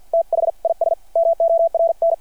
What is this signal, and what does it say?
Morse code, heard as a keyed tone, about 53 wpm, spaced as for 45 wpm, 660 Hz. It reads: THIS MORN
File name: archived / CRSnet